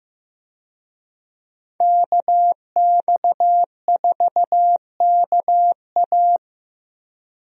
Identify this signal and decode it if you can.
Morse code, heard as a keyed tone, about 16 wpm, 700 Hz. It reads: KX4KA
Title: USA Call Signs - 5 Characters - Repeated 2x - 15wpm